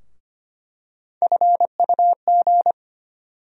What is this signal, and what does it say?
Morse code, heard as a keyed tone, about 25 wpm, 700 Hz. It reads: FUG
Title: USA Call Sign Postfixes - 3 Letters 25wpm